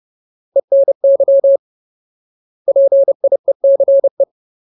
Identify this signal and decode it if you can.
Morse code, heard as a keyed tone, about 30 wpm, 550 Hz. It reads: ENY PIECE
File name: Ultimate Word List / 30WPM Ultimate